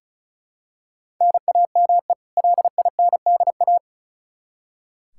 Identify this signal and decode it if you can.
Morse code, heard as a keyed tone, about 36 wpm, 700 Hz. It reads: NAME LINDA
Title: Practice QSO - Exchange 1 - Beginning - 35wpm